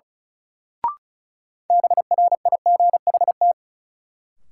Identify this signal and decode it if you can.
Morse code, heard as a keyed tone, about 36 wpm, 700 Hz. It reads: BRIGHT
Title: Sentences from Top 1000 Words - New Word Drill - 35wpm